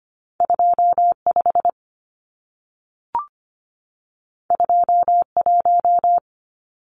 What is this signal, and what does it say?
Morse code, heard as a keyed tone, about 25 wpm, 700 Hz. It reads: 25 21
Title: Numbers - 2-Digit 25wpm